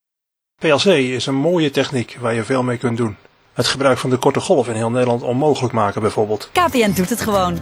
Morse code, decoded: none